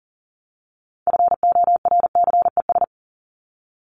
Morse code, heard as a keyed tone, about 40 wpm, 700 Hz. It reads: FORCES
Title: ES:S Suffix - Top 100 Words with Base Words 40wpm